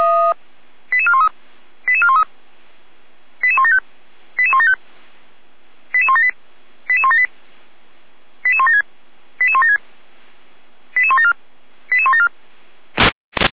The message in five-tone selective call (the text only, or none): 88413 88413 88175 88175 88178 88178 88176 88176 88174 88174